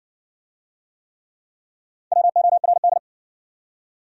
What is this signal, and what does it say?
Morse code, heard as a keyed tone, about 60 wpm, 700 Hz. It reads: WORD